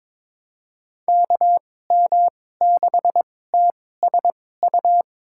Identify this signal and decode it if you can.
Morse code, heard as a keyed tone, about 22 wpm, spaced as for 16 wpm, 700 Hz. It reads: KM6TSU
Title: USA Call Signs - 6 Characters - 2x Character Spacing - 22wpm